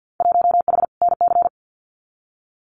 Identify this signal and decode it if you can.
Morse code, heard as a keyed tone, about 48 wpm, 700 Hz. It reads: 1H NC
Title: Winter Field Day Exchanges 50wpm